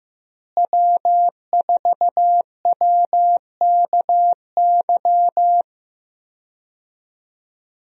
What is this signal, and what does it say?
Morse code, heard as a keyed tone, about 15 wpm, 700 Hz. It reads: W4WKY
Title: USA Call Signs - 5 Characters - Repeated 2x - 15wpm